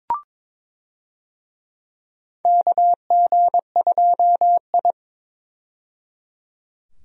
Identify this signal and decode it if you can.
Morse code, heard as a keyed tone, about 21 wpm, 700 Hz. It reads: KG2I
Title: USA Call Signs - 4 Characters - 22wpm